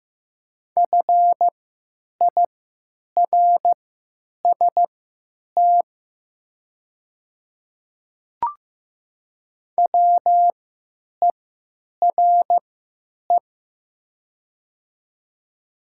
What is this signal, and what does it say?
Morse code, heard as a keyed tone, about 15 wpm, spaced as for 9 wpm, 700 Hz. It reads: FIRST WERE
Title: 3x Character Spacing - Top 100 Words - 15wpm